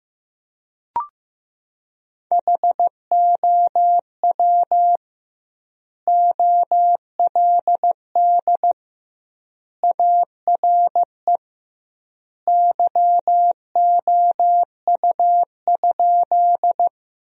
Morse code, heard as a keyed tone, about 15 wpm, 700 Hz. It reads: HOW OLD ARE YOU?
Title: 2x Word-Spacing - Sentences from Top 200 Words - 15wpm